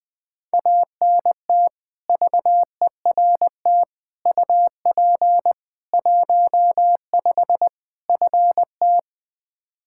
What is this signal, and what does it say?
Morse code, heard as a keyed tone, about 21 wpm, 700 Hz. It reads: ANT VERT UP 15 FT
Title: Practice QSO - Exchange 2 - Info - 20wpm